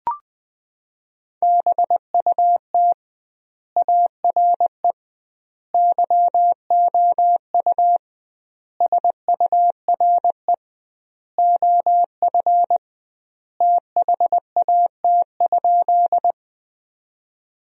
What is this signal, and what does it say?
Morse code, heard as a keyed tone, about 21 wpm, 700 Hz. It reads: BUT ARE YOU SURE OF THAT?
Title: 2x Word-Spacing - Sentences from Top 300 Words - 20wpm